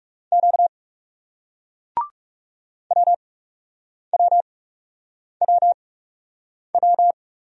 Morse code, heard as a keyed tone, about 40 wpm, spaced as for 8 wpm, 700 Hz. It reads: Q WWWW